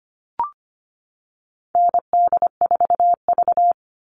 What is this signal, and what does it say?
Morse code, heard as a keyed tone, about 25 wpm, 700 Hz. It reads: ND4V
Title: Call Signs Repeated 3x - 25wpm